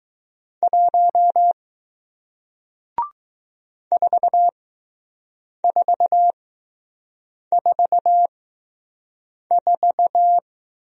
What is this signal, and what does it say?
Morse code, heard as a keyed tone, about 20 wpm, spaced as for 6 wpm, 700 Hz. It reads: 1 4444